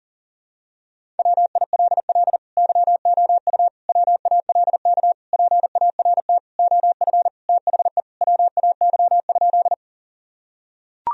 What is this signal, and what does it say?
Morse code, heard as a keyed tone, about 37 wpm, 700 Hz. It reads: WILL YOU WALK PART OF THE WAY?
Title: Sentences from Top 1000 Words - Review Part 1 - 40wpm